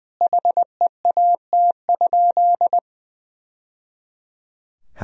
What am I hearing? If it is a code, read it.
Morse code, heard as a keyed tone, about 21 wpm, 700 Hz. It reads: HEAT?
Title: Sentences from Top 1000 Words - Review Part 3 - 20wpm